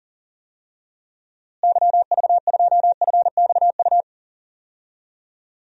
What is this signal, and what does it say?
Morse code, heard as a keyed tone, about 37 wpm, 700 Hz. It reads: YV2FXU
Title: DXCC-100 Call Signs - 6 Characters - Repeated 2x - 40wpm